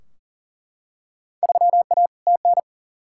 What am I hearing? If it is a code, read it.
Morse code, heard as a keyed tone, about 40 wpm, 700 Hz. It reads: 3A TN
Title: Field Day Exchanges - 40wpm